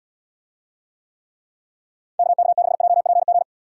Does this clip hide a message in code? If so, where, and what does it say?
Yes: Morse code from 2.2 to 3.4 s, heard as a keyed tone, about 72 wpm, 700 Hz. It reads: KK6JPG